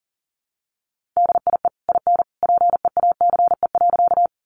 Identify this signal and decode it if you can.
Morse code, heard as a keyed tone, about 40 wpm, 700 Hz. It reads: DIE IN PEACE.
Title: Sentences from Top 1500 Words 40wpm